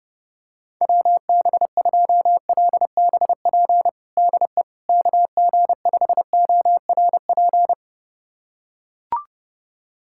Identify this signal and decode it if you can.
Morse code, heard as a keyed tone, about 32 wpm, 690 Hz. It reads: WB2LBP DE KG5ORP
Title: USA Callsign DE Callsign - 6 Characters - 30wpm